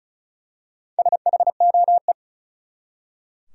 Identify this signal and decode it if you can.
Morse code, heard as a keyed tone, about 34 wpm, 700 Hz. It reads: SHOE
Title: Four Letter Words - 35wpm